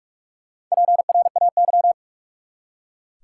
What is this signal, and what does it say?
Morse code, heard as a keyed tone, about 46 wpm, 700 Hz. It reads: PRAY
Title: Four Letter Words - 45wpm